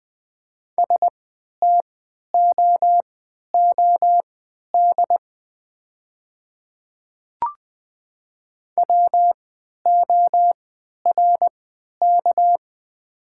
Morse code, heard as a keyed tone, about 20 wpm, spaced as for 11 wpm, 700 Hz. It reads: STOOD WORK
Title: 3x Character Spacing - Top 500 Words - 20wpm